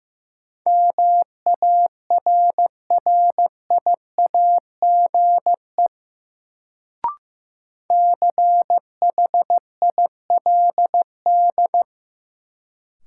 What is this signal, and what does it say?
Morse code, heard as a keyed tone, about 15 wpm, 700 Hz. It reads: MARRIAGE CHILD
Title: Sentences from Top 800 Words - New Word Drill - 15wpm